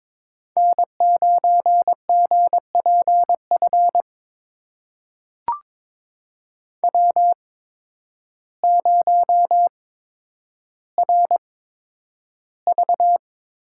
Morse code, heard as a keyed tone, about 21 wpm, spaced as for 10 wpm, 700 Hz. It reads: N9GPF W0RV
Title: USA Call Signs - 5 Characters - 8x Character Spacing - 22wpm